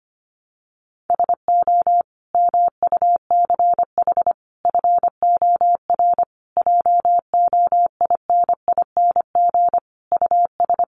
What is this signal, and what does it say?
Morse code, heard as a keyed tone, about 25 wpm, 700 Hz. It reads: SO MUCH FOR JOINING US
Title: Sets of 5 Words 25wpm